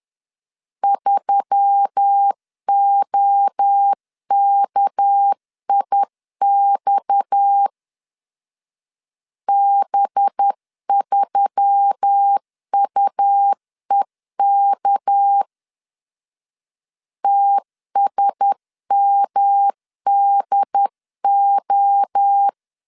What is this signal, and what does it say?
Morse code, heard as a keyed tone, about 10 wpm, 800 Hz. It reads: 3OKIX B3UEK TSMDO
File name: CW / DET / 1 6 1500C S C 10wpm 4252